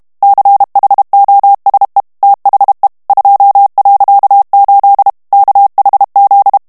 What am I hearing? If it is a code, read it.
Morse code, heard as a keyed tone, about 32 wpm, 800 Hz. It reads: CHOSE THE 2.8 KHZ
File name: ANGLAIS texte clair 3 vitesse 30 mots minute